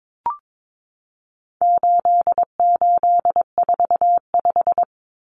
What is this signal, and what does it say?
Morse code, heard as a keyed tone, about 22 wpm, 700 Hz. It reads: 8845
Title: Numbers - 4-Digit 22wpm